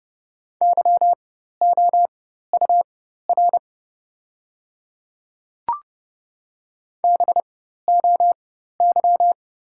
Morse code, heard as a keyed tone, about 30 wpm, spaced as for 14 wpm, 700 Hz. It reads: YOUR BOY
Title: 4x Character Spacing - Top 300 Words - 30wpm